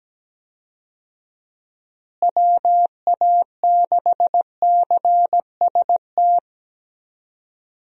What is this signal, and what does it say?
Morse code, heard as a keyed tone, about 17 wpm, 700 Hz. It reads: WA6CST
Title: USA Call Signs - 6 Characters - Repeated 3x - 17wpm